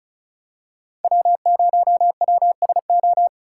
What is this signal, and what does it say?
Morse code, heard as a keyed tone, about 36 wpm, 700 Hz. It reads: W0WSO
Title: USA Call Signs - 4x Character Spacing - 35wpm